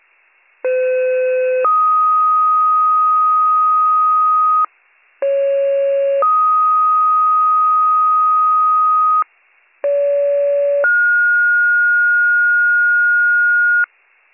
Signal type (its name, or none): two-tone page